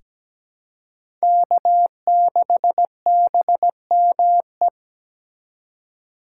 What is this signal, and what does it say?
Morse code, heard as a keyed tone, about 17 wpm, 700 Hz. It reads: K6BME
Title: USA Call Signs - 3x Character Spacing - 17wpm